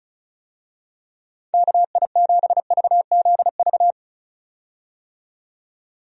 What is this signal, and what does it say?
Morse code, heard as a keyed tone, about 36 wpm, 700 Hz. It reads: KI7VZV
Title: USA Call Signs - 6 Characters - 35wpm